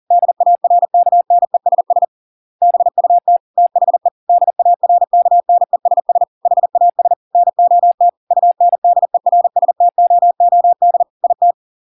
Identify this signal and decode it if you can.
Morse code, heard as a keyed tone, about 40 wpm, 700 Hz. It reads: DARKNESS BUT THE DARKNESS HAS NOT UNDERSTOOD IT